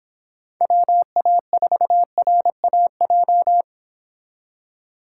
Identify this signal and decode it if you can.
Morse code, heard as a keyed tone, about 26 wpm, 700 Hz. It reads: WA4RAJ